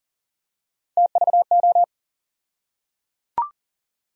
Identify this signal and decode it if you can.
Morse code, heard as a keyed tone, about 38 wpm, 700 Hz. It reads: TVO
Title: USA Call Sign Postfixes - 3 Letters 40wpm